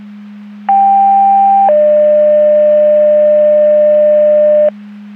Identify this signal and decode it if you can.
Two-tone page, heard as a one-second tone, then a three-second tone, about 790 then 600 Hz, with noise. It is heard from 0.7 to 4.7 s.
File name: Portage 154.385 Paging